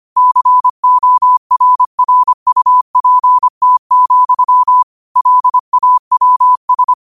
Morse code, heard as a keyed tone, about 25 wpm, 1000 Hz. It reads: CORRUPT, LAWS
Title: 25 WPM morse code quotes for Thu, 29 Feb 2024